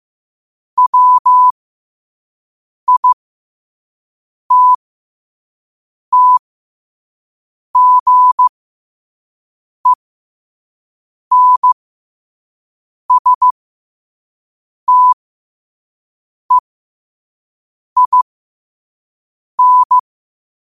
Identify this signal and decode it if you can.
Morse code, heard as a keyed tone, about 15 wpm, spaced as for 5 wpm, 1000 Hz. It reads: WITTGENSTEIN
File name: qotdfiles / 5wpm